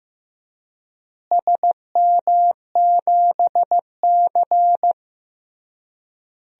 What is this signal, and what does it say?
Morse code, heard as a keyed tone, about 15 wpm, 700 Hz. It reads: SM7C
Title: DXCC-100 Call Signs - 4 Characters - Repeated 3x - 15wpm